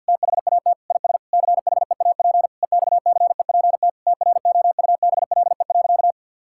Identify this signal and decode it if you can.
Morse code, heard as a keyed tone, about 52 wpm, 700 Hz. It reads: THAT IS CHEAP EXCEPT TROUBLE.
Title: Sentences from Top 1600 Words 50wpm